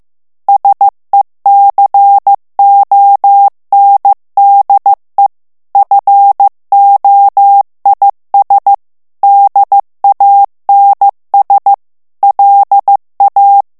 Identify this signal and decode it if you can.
Morse code, heard as a keyed tone, about 15 wpm, 800 Hz. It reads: SECONDE FOIS DANS LA
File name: FRANCAIS TEXTE CLAIR 2 VITESSE 14 MOTS MINUTES